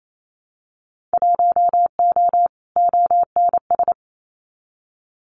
Morse code, heard as a keyed tone, about 28 wpm, 690 Hz. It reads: 1O ONS
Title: Winter Field Day Exchanges Encore 28wpm